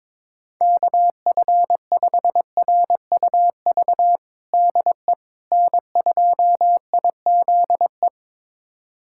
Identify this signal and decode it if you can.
Morse code, heard as a keyed tone, about 21 wpm, 700 Hz. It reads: KF5RUV DE N2IZE
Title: USA Callsign DE Callsign 22wpm